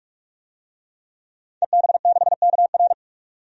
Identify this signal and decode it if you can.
Morse code, heard as a keyed tone, about 46 wpm, 700 Hz. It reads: EB6KR
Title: DXCC-100 Call Signs - 5 Characters - 45wpm